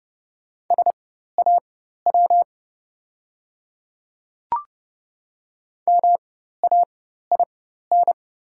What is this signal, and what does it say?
Morse code, heard as a keyed tone, about 31 wpm, spaced as for 14 wpm, 690 Hz. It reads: SAW MAIN
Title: 4x Character Spacing - Top 300 Words - 30wpm